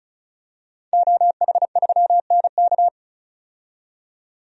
Morse code, heard as a keyed tone, about 34 wpm, 700 Hz. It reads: OH3NK
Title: DXCC-100 Call Signs - 5 Characters - 35wpm